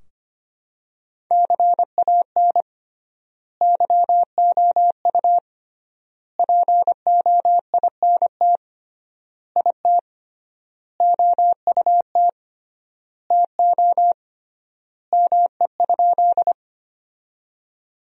Morse code, heard as a keyed tone, about 25 wpm, 700 Hz. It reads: CAN YOU POINT IT OUT TO ME?
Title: 3x Word-Spacing - Sentences from Top 200 Words - 25wpm